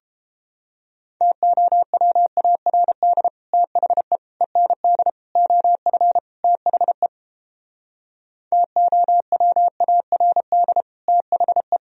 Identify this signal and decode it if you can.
Morse code, heard as a keyed tone, about 31 wpm, 700 Hz. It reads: TOWARD THE END OF THE TOWARD THE